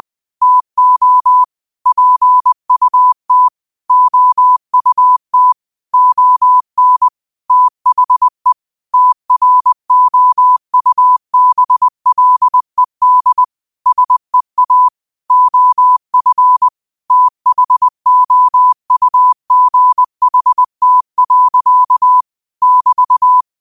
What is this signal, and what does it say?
Morse code, heard as a keyed tone, about 20 wpm, 1000 Hz. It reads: TO PUT OUT ON THE TROUBLED SEA OF THOUGHT. <BT>
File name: qotdfiles / 20wpm